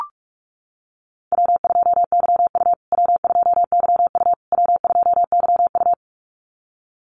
Morse code, heard as a keyed tone, about 44 wpm, 700 Hz. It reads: W2YU W2YU W2YU
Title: USA Call Signs Repeated 3x 45wpm